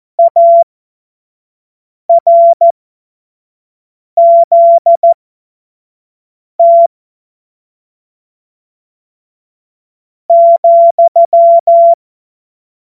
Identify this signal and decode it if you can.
Morse code, heard as a keyed tone, about 14 wpm, spaced as for 4 wpm, 680 Hz. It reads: ARZT,
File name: Morsekurs / Witze / Witz 31. August 2021, 25 F70